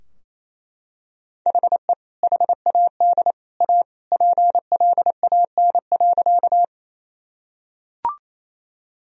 Morse code, heard as a keyed tone, about 28 wpm, 700 Hz. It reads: HE HAD A PLAN.
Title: Sentences from Top 1000 Words - Review Part 2 - 28wpm